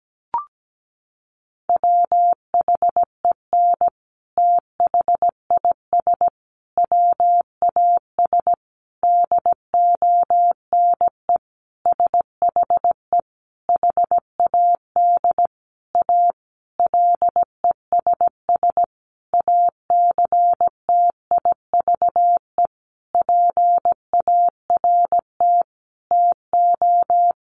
Morse code, heard as a keyed tone, about 17 wpm, 700 Hz. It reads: WHEN THIS WAS DONE SHE HAD A LESS ACTIVE PART TO
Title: Sentences from Top 900 Words 17wpm